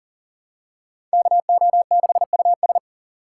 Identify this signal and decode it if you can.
Morse code, heard as a keyed tone, about 40 wpm, 700 Hz. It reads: KO6US
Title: USA Call Signs - 6x Character Spacing - 40wpm